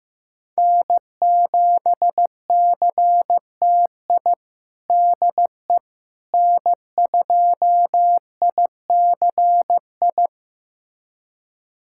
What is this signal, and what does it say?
Morse code, heard as a keyed tone, about 15 wpm, 700 Hz. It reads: N7CTI DE N2ICI
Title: USA Callsign DE Callsign - 5 Characters - 15wpm